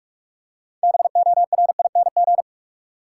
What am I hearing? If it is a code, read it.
Morse code, heard as a keyed tone, about 42 wpm, 700 Hz. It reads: BORING
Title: Six Letter Words - 45wpm